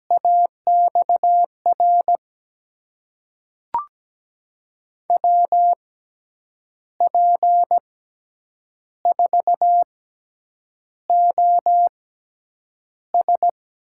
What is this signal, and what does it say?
Morse code, heard as a keyed tone, about 17 wpm, spaced as for 7 wpm, 700 Hz. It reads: AXR WP4OS
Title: USA Call Signs - 5 Characters - 6x Character Spacing - 17wpm